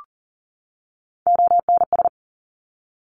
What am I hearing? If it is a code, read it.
Morse code, heard as a keyed tone, about 42 wpm, 700 Hz. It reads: ONS
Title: Top Three Letter Combinations in Words 40wpm